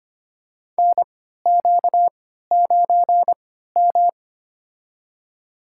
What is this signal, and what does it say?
Morse code, heard as a keyed tone, about 25 wpm, spaced as for 14 wpm, 700 Hz. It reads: NQ9M
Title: USA Call Signs - 3x Character Spacing - 25wpm